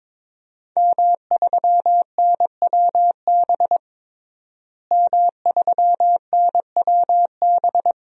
Morse code, heard as a keyed tone, about 22 wpm, 700 Hz. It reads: M3NWB M3NWB
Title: DXCC-100 Call Signs - 5 Characters - Repeated 3x - 22wpm